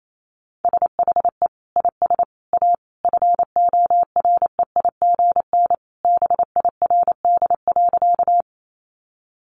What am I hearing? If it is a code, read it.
Morse code, heard as a keyed tone, about 28 wpm, 700 Hz. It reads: SHE IS A FOREIGN BIRD.
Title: Sentences from Top 1400 Words 28wpm